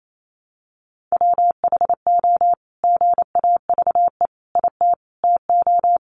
Morse code, heard as a keyed tone, about 28 wpm, 700 Hz. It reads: WHO GAVE IT TO
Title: Sentences from Top 500 Words 28wpm